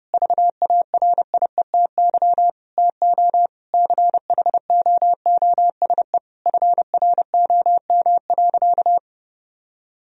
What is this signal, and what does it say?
Morse code, heard as a keyed tone, about 30 wpm, 700 Hz. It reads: VARIETY TO CHOOSE FROM.
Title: Sentences from Top 1000 Words - Review Part 4 - 30wpm